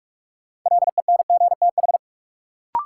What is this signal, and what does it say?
Morse code, heard as a keyed tone, about 47 wpm, 700 Hz. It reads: LENGTH
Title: Sentences from Top 1000 Words - New Word Drill - 45wpm